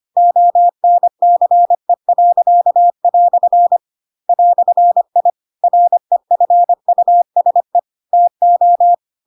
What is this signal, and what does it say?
Morse code, heard as a keyed tone, about 25 wpm, 700 Hz. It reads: ONCE." "I REFUSE TO
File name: Pinocchio / Pinocchio ch4 25 25 ik1hge